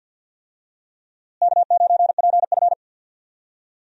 Morse code, heard as a keyed tone, about 53 wpm, 700 Hz. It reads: K9PF
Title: USA Call Signs - 4 Characters - Repeated 3x - 50wpm